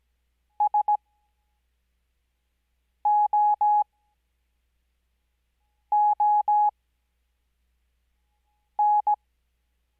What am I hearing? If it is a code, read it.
Morse code, heard as a keyed tone, about 17 wpm, spaced as for 4 wpm, 840 Hz. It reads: SOON
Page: Farnsworth Spacing: 15WPM Character speed Spaced at 5WPM.